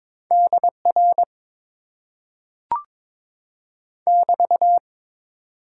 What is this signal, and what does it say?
Morse code, heard as a keyed tone, about 22 wpm, 700 Hz. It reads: DR <BT>
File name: Top 90 QSO Elements - 8x Character Spacing - 22wpm